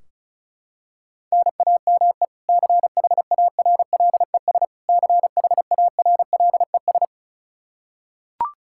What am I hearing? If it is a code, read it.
Morse code, heard as a keyed tone, about 34 wpm, 700 Hz. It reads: NAME CHARLES CHARLES
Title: Practice QSO - Exchange 1 - Beginning - 35wpm